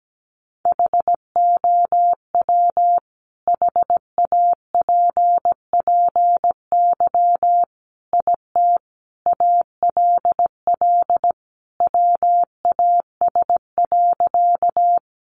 Morse code, heard as a keyed tone, about 17 wpm, 700 Hz. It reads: HOW HAPPY IT ALL WAS.
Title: Sentences from Top 1000 Words 17wpm